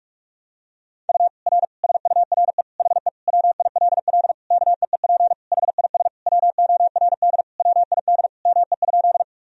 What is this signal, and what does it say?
Morse code, heard as a keyed tone, about 41 wpm, 700 Hz. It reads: U R SURE HE WILL KEEP HIS WORD WID ME?
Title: Sentences from Top 300 Words and Abbreviations 45wpm